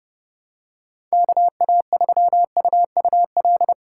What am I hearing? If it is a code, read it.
Morse code, heard as a keyed tone, about 30 wpm, 700 Hz. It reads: KA3UUL
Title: USA Call Signs - 6 Characters - 2x Character Spacing - 30wpm